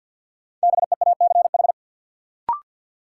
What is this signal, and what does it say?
Morse code, heard as a keyed tone, about 50 wpm, 700 Hz. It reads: BEACH
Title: Sentences from Top 1500 Words - New Word Drill - 50wpm